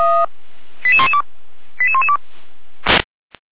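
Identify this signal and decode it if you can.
Five-tone selective call (ZVEI), heard as quick runs of five tones, tones of about 70 ms. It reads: 88112 88112
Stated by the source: ZVEI five-tone selective call received by SDR.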